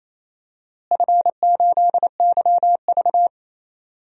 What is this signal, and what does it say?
Morse code, heard as a keyed tone, about 27 wpm, 700 Hz. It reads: F8YV
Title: DXCC-100 Call Signs - 4 Characters - Repeated 3x - 28wpm